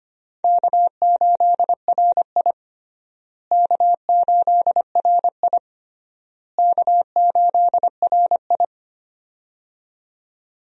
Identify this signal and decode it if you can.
Morse code, heard as a keyed tone, about 25 wpm, 700 Hz. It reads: K8RI K8RI K8RI
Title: USA Call Signs - 4 Characters - Repeated 3x - 25wpm